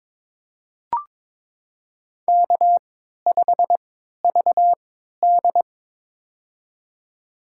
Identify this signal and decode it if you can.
Morse code, heard as a keyed tone, about 22 wpm, spaced as for 13 wpm, 700 Hz. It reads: K5VD
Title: USA Call Signs - 4 Characters - 3x Character Spacing - 22wpm